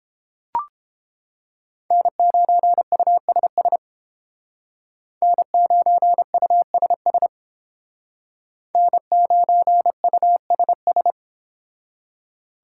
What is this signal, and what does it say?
Morse code, heard as a keyed tone, about 30 wpm, 700 Hz. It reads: N9USS N9USS N9USS